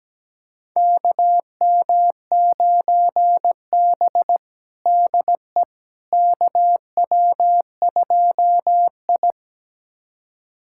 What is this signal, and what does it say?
Morse code, heard as a keyed tone, about 17 wpm, 700 Hz. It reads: KM9B DE KW2I
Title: USA Callsign DE Callsign - 4 Characters - 17wpm